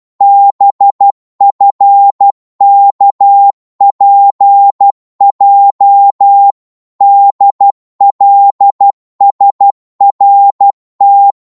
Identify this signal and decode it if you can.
Morse code, heard as a keyed tone, about 12 wpm, 800 Hz. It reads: BFKPJ DLSRT
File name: Morse / ltr-12wpm